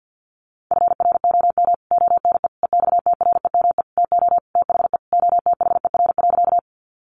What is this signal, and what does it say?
Morse code, heard as a keyed tone, about 50 wpm, 700 Hz. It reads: FROM ONE EXTREME TO THE OTHER.
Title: Sentences from Top 1300 Words 50wpm